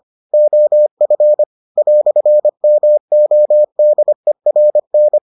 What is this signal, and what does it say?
Morse code, heard as a keyed tone, about 25 wpm, 600 Hz. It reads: OF "MODERN